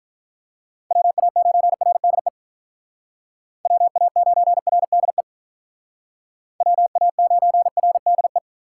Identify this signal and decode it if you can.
Morse code, heard as a keyed tone, about 41 wpm, 700 Hz. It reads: WA9RDE WA9RDE WA9RDE
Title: Speed Racing - USA 2x3 Call Signs 35wpm